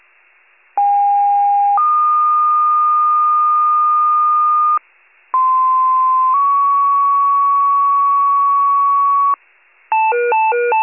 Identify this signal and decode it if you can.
Two-tone page, heard as a one-second tone, then a three-second tone, about 810 then 1220 Hz, about 1040 then 1120 Hz. It is heard from 0.8 to 9.3 s.
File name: wcalert